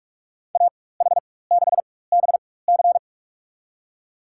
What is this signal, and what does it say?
Morse code, heard as a keyed tone, about 47 wpm, spaced as for 21 wpm, 700 Hz. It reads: AH6BC
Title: USA Call Signs - 5 Characters - 4x Character Spacing - 45wpm